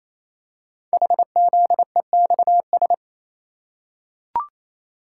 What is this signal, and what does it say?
Morse code, heard as a keyed tone, about 29 wpm, 700 Hz. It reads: HZEXS
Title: Code Groups - 5 Letters - 28wpm